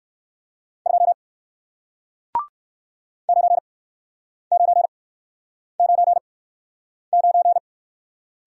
Morse code, heard as a keyed tone, about 62 wpm, spaced as for 9 wpm, 700 Hz. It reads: 2 9999